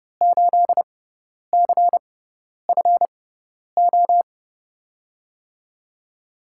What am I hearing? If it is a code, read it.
Morse code, heard as a keyed tone, about 32 wpm, spaced as for 10 wpm, 700 Hz. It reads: 8CFO
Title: USA Call Signs - 5 Characters - 6x Character Spacing - 30wpm